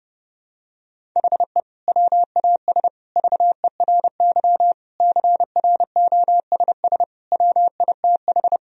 Morse code, heard as a keyed tone, about 32 wpm, 700 Hz. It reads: HE WAS VERY CROSS WITH
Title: Sentences from Top 1000 Words - Review Part 1 - 30wpm